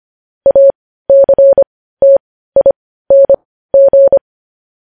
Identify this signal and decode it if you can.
Morse code, heard as a keyed tone, about 25 wpm, spaced as for 15 wpm, 550 Hz. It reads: ACTING
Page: News headlines in morse code at 15 wpm